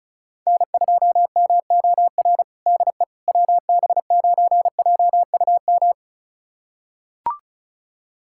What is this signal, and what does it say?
Morse code, heard as a keyed tone, about 36 wpm, 700 Hz. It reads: N2MOR DE WB9JUM
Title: USA Callsign DE Callsign 35wpm